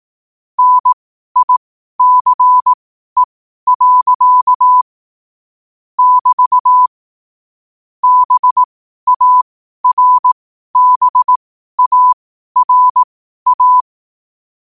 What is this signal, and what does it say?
Morse code, heard as a keyed tone, about 18 wpm, spaced as for 12 wpm, 1000 Hz. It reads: NICE. <BT> BARBARA
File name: qotdfiles / 12wpm